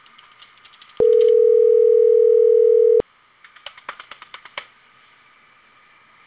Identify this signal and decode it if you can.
Ringback tone (North America), 440 + 480 Hz.